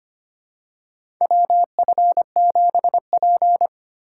Morse code, heard as a keyed tone, about 25 wpm, 700 Hz. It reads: WF7P
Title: USA Call Signs - 4 Characters - 3x Character Spacing - 25wpm